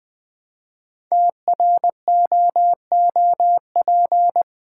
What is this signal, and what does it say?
Morse code, heard as a keyed tone, about 20 wpm, 700 Hz. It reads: TROOP
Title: Build A Word - 1 Letter At A Time - 5-Letter Words - 20wpm